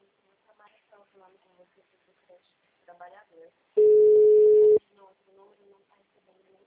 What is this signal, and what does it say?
Ringback tone (Europe, 425 Hz), with speech.